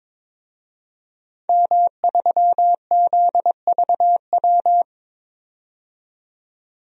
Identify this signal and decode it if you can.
Morse code, heard as a keyed tone, about 22 wpm, 700 Hz. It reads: M3ZVW
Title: DXCC-100 Call Signs - 5 Characters - 22wpm